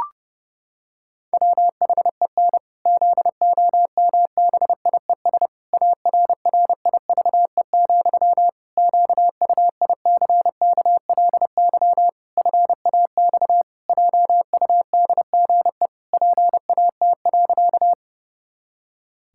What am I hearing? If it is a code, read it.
Morse code, heard as a keyed tone, about 30 wpm, 700 Hz. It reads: WHEN ZOMBIES ARRIVE, QUICKLY FAX JUDGE PAT.
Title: Pangram Challenge v2 - 30wpm